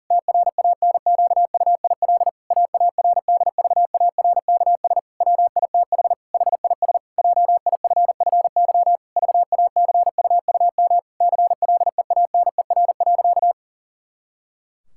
Morse code, heard as a keyed tone, about 40 wpm, 700 Hz. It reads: TRANQUIL AARDVARKS WITH HIS JIFFY VACUUM CLEANER.
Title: Pangram Challenge v2 - 40wpm